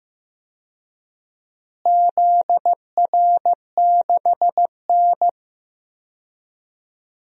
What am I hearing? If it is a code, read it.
Morse code, heard as a keyed tone, about 15 wpm, 700 Hz. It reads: ZR6N
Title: DXCC-100 Call Signs - 4 Characters - Repeated 3x - 15wpm